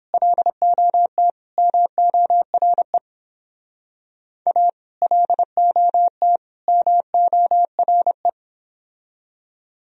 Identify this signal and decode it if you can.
Morse code, heard as a keyed tone, about 30 wpm, 700 Hz. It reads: LOT MORE A LOT MORE